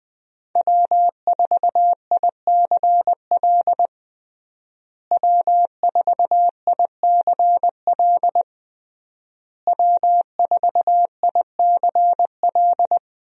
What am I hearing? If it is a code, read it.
Morse code, heard as a keyed tone, about 20 wpm, 700 Hz. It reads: W4ICL W4ICL W4ICL
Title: USA Call Signs - 5 Characters - Repeated 3x - 20wpm